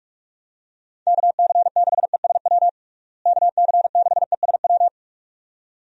Morse code, heard as a keyed tone, about 47 wpm, 700 Hz. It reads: KC6ESW KC6ESW
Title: USA Call Signs - 6 Characters - Repeated 2x - 45wpm